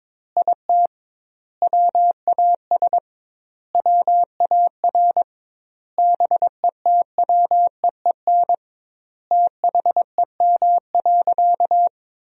Morse code, heard as a keyed tone, about 23 wpm, 700 Hz. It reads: IT WAS WAR BETWEEN THEM.
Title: 2x Word-Spacing - Sentences from Top 400 Words - 22wpm